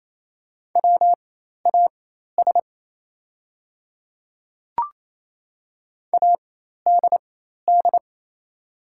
Morse code, heard as a keyed tone, about 27 wpm, spaced as for 13 wpm, 690 Hz. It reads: WAS ADD
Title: 4x Character Spacing - Top 200 Words - 28wpm